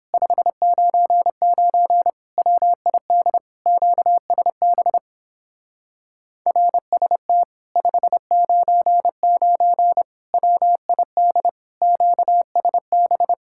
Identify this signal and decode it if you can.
Morse code, heard as a keyed tone, about 31 wpm, 700 Hz. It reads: 599 WID QSB RST 599 WID QSB